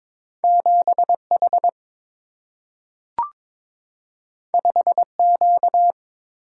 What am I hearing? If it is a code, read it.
Morse code, heard as a keyed tone, about 22 wpm, 700 Hz. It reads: 7 H 5 Q
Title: Single-Letter Single-Number 22wpm